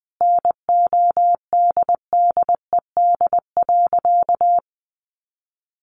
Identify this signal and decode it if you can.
Morse code, heard as a keyed tone, about 20 wpm, 700 Hz. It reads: NODDED.
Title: A Sound Of Thunder 20wpm